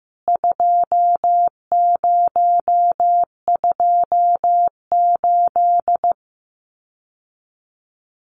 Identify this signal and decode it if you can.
Morse code, heard as a keyed tone, about 15 wpm, 700 Hz. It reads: 2028
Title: CWT Exchange 15wpm